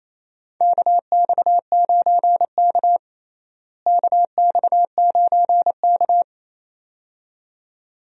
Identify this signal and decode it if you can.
Morse code, heard as a keyed tone, about 28 wpm, 700 Hz. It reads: KX9K KX9K
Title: USA Call Signs - 4 Characters - Repeated 2x - 28wpm